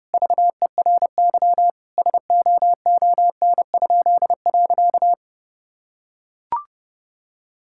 Morse code, heard as a keyed tone, about 30 wpm, 700 Hz. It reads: VERY SOON?.
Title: Pangram Challenge v2 - 30wpm